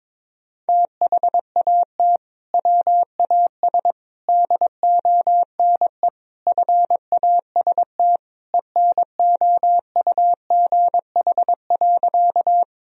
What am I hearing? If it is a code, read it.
Morse code, heard as a keyed tone, about 22 wpm, 700 Hz. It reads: THAT WAS DONE FAST ENOUGH.
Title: Sentences from Top 1000 Words - Review Part 2 - 22wpm